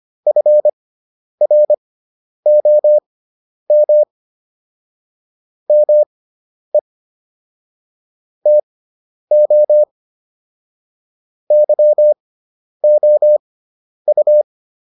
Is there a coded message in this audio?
Yes: Morse code from 0.3 to 14.4 s, heard as a keyed tone, about 25 wpm, spaced as for 10 wpm, 600 Hz. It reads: FROM ME TO YOU